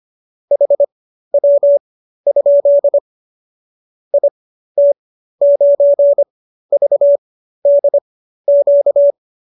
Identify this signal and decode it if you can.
Morse code, heard as a keyed tone, about 25 wpm, spaced as for 13 wpm, 570 Hz. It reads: HW? IT9VDQ